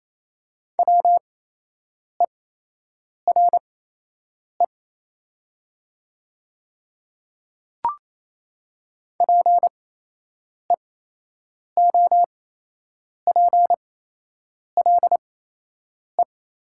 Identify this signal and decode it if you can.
Morse code, heard as a keyed tone, about 28 wpm, spaced as for 8 wpm, 700 Hz. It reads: WERE PEOPLE
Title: 8x Character Spacing - Top 100 Words - 28wpm